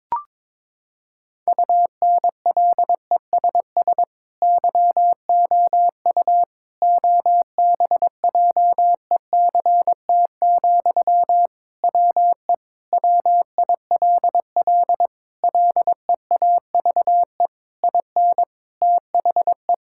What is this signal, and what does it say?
Morse code, heard as a keyed tone, about 22 wpm, 700 Hz. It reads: UNLESS YOU OBJECT, WE WILL LEAVE IN THE
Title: Sentences from Top 1000 Words - Review Part 6 - 22wpm